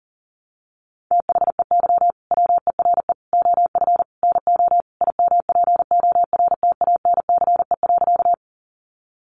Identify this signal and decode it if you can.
Morse code, heard as a keyed tone, about 40 wpm, 700 Hz. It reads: THEY WERE OF NO IMPORTANCE.
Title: Sentences from Top 1200 Words 40wpm